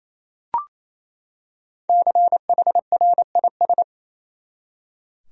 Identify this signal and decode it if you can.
Morse code, heard as a keyed tone, about 28 wpm, 700 Hz. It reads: CHRIS
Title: USA Common Names - 28wpm